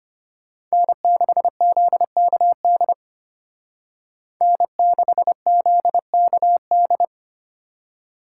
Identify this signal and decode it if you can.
Morse code, heard as a keyed tone, about 33 wpm, 700 Hz. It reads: N6ZKD N6ZKD